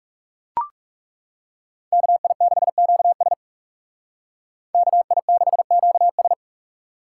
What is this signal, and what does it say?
Morse code, heard as a keyed tone, about 34 wpm, 700 Hz. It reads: KI6QS KI6QS
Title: Speed Racing - USA 2x2 Call Signs 30wpm